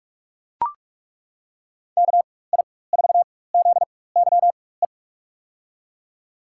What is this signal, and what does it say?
Morse code, heard as a keyed tone, about 41 wpm, spaced as for 20 wpm, 700 Hz. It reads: KI4ZYE
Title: USA Call Signs - 6 Characters - 4x Character Spacing - 45wpm